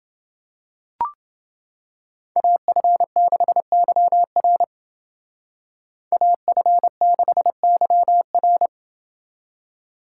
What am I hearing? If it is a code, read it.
Morse code, heard as a keyed tone, about 29 wpm, 700 Hz. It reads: AF6YR AF6YR